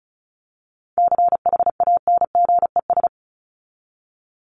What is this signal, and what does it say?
Morse code, heard as a keyed tone, about 34 wpm, 700 Hz. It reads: CHANGES
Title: ES:S Suffix - Top 100 Words with Base Words 35wpm